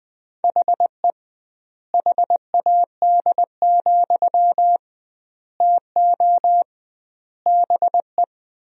Morse code, heard as a keyed tone, about 20 wpm, 700 Hz. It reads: HE HAD, TO BE
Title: 2x Word-Spacing - Sentences from Top 500 Words - 20wpm